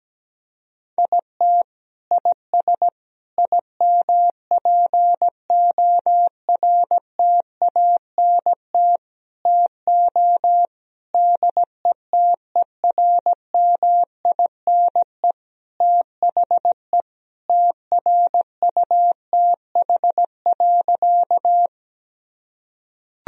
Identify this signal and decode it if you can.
Morse code, heard as a keyed tone, about 17 wpm, 700 Hz. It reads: IT IS IMPORTANT TO DETERMINE THE TRUTH.
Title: Sentences from Top 1000 Words - Review Part 4 - 17wpm